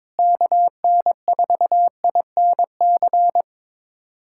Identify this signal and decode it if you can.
Morse code, heard as a keyed tone, about 22 wpm, 700 Hz. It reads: KN4INC
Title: USA Callsign DE Callsign - 6 Characters - 22wpm